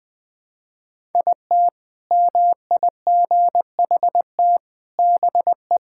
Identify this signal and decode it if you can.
Morse code, heard as a keyed tone, about 20 wpm, 700 Hz. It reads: IT MIGHT BE
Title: Sentences with 4-Word Phrases - 20wpm